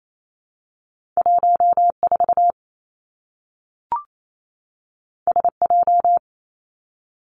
Morse code, heard as a keyed tone, about 28 wpm, 700 Hz. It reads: 1 4 S J
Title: Two Challenging Letter-Number Combinations 28wpm